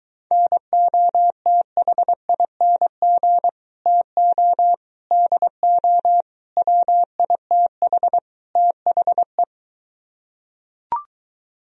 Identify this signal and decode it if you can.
Morse code, heard as a keyed tone, about 23 wpm, 700 Hz. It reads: NOTHING TO DO WITH THE